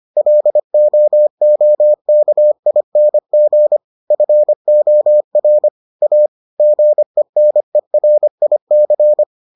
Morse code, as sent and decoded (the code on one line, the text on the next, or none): .-.. --- --- -.- .. -. --. / ..-. --- .-. / .- / --. -. .-. .. -.-.
LOOKING FOR A GENERIC